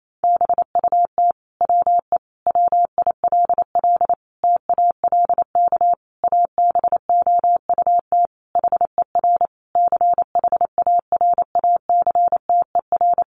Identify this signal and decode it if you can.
Morse code, heard as a keyed tone, about 28 wpm, 700 Hz. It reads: BUT WE WILL TALK ABOUT HER CHARACTER
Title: Sentences from Top 1200 Words 28wpm